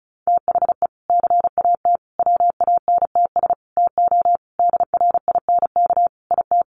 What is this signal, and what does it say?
Morse code, heard as a keyed tone, about 34 wpm, 700 Hz. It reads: THE CAT WANTS TO DRINK IT
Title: Additional Easy Phrases 35wpm